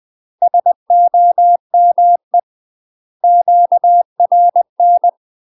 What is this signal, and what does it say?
Morse code, heard as a keyed tone, about 20 wpm, 700 Hz. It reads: SOME QRN
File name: UK MASTER QSOs TRAINING / 20WPM / UK MASTER QSOs 0019